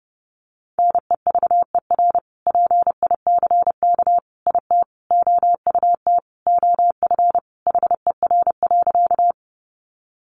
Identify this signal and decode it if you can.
Morse code, heard as a keyed tone, about 30 wpm, 700 Hz. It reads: NEVER PICK IT OUT OF HER.
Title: Sentences from Top 1000 Words 30wpm